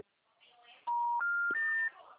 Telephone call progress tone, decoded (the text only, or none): special information tone